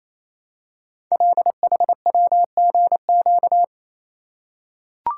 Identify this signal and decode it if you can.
Morse code, heard as a keyed tone, about 27 wpm, 700 Hz. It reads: LHWGQ
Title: Code Groups - 5 Letters - 28wpm